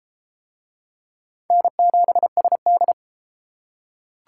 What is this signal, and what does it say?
Morse code, heard as a keyed tone, about 32 wpm, 700 Hz. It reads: N7SD